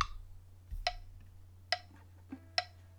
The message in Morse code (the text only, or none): none